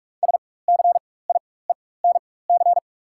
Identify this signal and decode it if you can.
Morse code, heard as a keyed tone, about 42 wpm, spaced as for 21 wpm, 690 Hz. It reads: SCIENC
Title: 4x Character Spacing - Top 500 Words - 45wpm